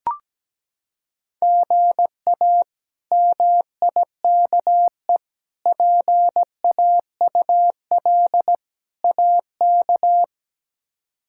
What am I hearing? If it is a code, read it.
Morse code, heard as a keyed tone, about 17 wpm, 700 Hz. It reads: GA MIKE PAUL AK
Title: K1USN SST QSO Practice - 17wpm